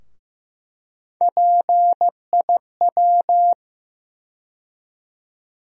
Morse code, heard as a keyed tone, about 15 wpm, 700 Hz. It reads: PIW
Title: USA Call Sign Postfixes - 3 Letters 15wpm